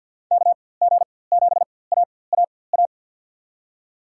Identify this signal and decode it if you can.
Morse code, heard as a keyed tone, about 51 wpm, spaced as for 23 wpm, 700 Hz. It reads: KG7AAA
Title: USA Call Signs - 6 Characters - 4x Character Spacing - 50wpm